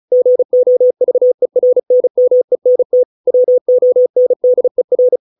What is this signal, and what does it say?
Morse code, heard as a keyed tone, about 35 wpm, 500 Hz. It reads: GOVERNMENT WONDER
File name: 632 words / 632 Words 35WPM